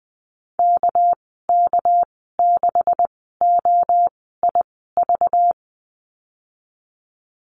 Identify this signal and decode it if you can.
Morse code, heard as a keyed tone, about 20 wpm, spaced as for 15 wpm, 700 Hz. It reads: KK6OIV
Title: 219 Lesson 35g - Number 0 - Call Signs 20wpm